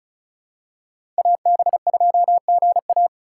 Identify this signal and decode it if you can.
Morse code, heard as a keyed tone, about 36 wpm, 700 Hz. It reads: AB2GA
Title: USA Call Signs - 5 Characters - 3x Character Spacing - 35wpm